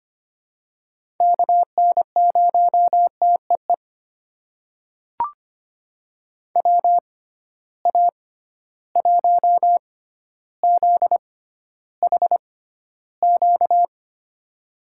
Morse code, heard as a keyed tone, about 25 wpm, spaced as for 13 wpm, 700 Hz. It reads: KN0TEE WA1ZHQ
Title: USA Call Signs - 6 Characters - 6x Character Spacing - 25wpm